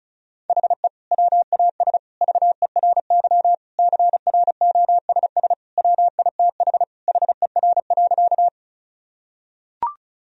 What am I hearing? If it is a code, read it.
Morse code, heard as a keyed tone, about 36 wpm, 700 Hz. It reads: HE WAS VERY CROSS WITH HER.
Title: Sentences from Top 1000 Words - Review Part 1 - 35wpm